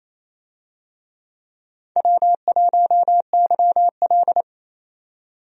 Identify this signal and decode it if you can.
Morse code, heard as a keyed tone, about 28 wpm, 700 Hz. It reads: W1YL
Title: USA Call Signs - 4 Characters - 28wpm